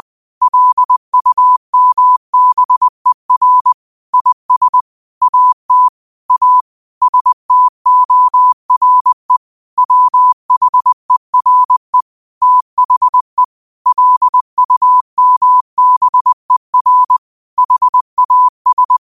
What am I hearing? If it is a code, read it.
Morse code, heard as a keyed tone, about 20 wpm, 1000 Hz. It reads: LUMBER IS AT A STORE WHERE THE LUMBER HAS